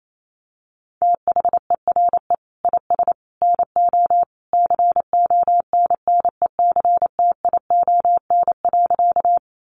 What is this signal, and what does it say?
Morse code, heard as a keyed tone, about 28 wpm, 700 Hz. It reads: THERE IS NO CONNECTION.
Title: Sentences from Top 1300 Words 28wpm